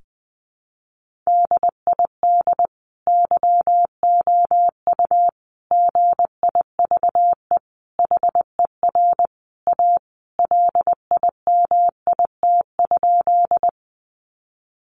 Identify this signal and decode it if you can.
Morse code, heard as a keyed tone, about 20 wpm, 700 Hz. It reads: DID YOU GIVE HER A LIMIT?
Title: Sentences from Top 1500 Words 20wpm